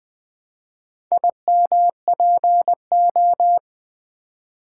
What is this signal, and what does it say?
Morse code, heard as a keyed tone, about 20 wpm, 700 Hz. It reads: IMPO
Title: Build A Word - 4 Letters At A Time - 10-Letter Words - 20wpm